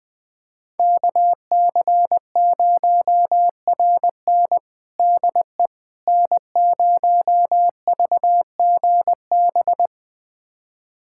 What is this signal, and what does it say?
Morse code, heard as a keyed tone, about 20 wpm, 700 Hz. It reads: KC0RN DE N0VGB
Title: USA Callsign DE Callsign 20wpm